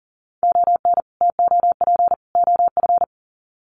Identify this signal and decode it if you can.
Morse code, heard as a keyed tone, about 40 wpm, 700 Hz. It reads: ON TOP OF
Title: Sets of 3 Words 40wpm